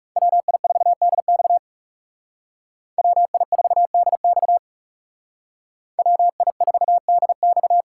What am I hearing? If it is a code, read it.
Morse code, heard as a keyed tone, about 37 wpm, 700 Hz. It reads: WI4DX WI4DX WI4DX
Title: Speed Racing - USA Call Signs 30wpm